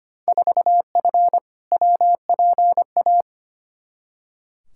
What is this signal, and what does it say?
Morse code, heard as a keyed tone, about 25 wpm, 700 Hz. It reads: 4F WPA
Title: Field Day Exchanges - 25wpm